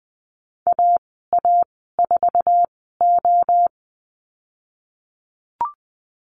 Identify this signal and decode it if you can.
Morse code, heard as a keyed tone, about 20 wpm, spaced as for 15 wpm, 700 Hz. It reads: AA4O
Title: 048 Lesson 9g - Number 4 - Call Signs 20wpm